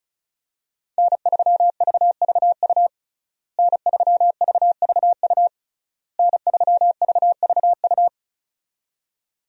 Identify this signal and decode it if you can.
Morse code, heard as a keyed tone, about 36 wpm, 700 Hz. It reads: N3VVU N3VVU N3VVU
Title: USA Call Signs - 5 Characters - Repeated 3x - 35wpm